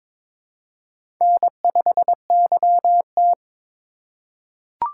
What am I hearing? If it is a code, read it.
Morse code, heard as a keyed tone, about 22 wpm, 700 Hz. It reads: N5YT
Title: USA Call Signs - 4x Character Spacing - 22wpm